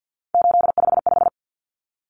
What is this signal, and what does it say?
Morse code, heard as a keyed tone, about 50 wpm, 700 Hz. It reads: 855
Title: Numbers - 3-Digit 50wpm